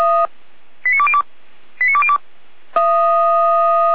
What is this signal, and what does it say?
Five-tone selective call (ZVEI), heard as quick runs of five tones, tones of about 70 ms. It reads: 89202 89202